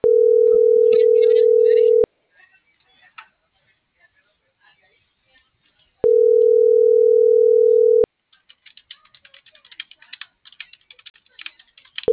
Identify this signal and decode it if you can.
Ringback tone (North America), 440 + 480 Hz.